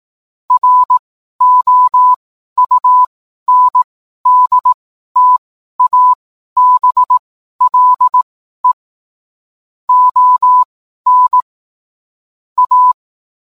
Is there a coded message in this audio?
Yes: Morse code from 0.5 to 12.9 s, heard as a keyed tone, about 18 wpm, spaced as for 12 wpm, 1000 Hz. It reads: ROUNDTABLE ON A